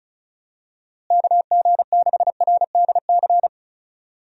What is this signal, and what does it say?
Morse code, heard as a keyed tone, about 36 wpm, 700 Hz. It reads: KG6RDC
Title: USA Call Signs - 6 Characters - Repeated 2x - 35wpm